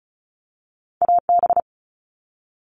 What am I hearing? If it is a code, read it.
Morse code, heard as a keyed tone, about 36 wpm, 700 Hz. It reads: AB
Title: Canadian Provinces and Territories - Repeated 3x - 35wpm